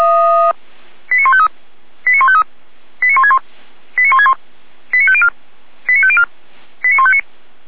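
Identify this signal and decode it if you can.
Five-tone selective call (ZVEI), heard as quick runs of five tones, tones of about 70 ms. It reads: 89153 89153 89161 89161 89504 89504 89159
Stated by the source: ZVEI five-tone call received by SDR, from the feuerwehr-orlamuende.org fire brigade audio server.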